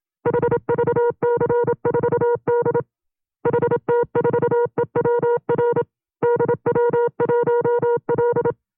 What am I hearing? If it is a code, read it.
Morse code, heard as a keyed tone, about 27 wpm, 480 Hz. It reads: HVC4D HT4EWR DW1L
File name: FreeFiles / CW-practice / 25WPM-side2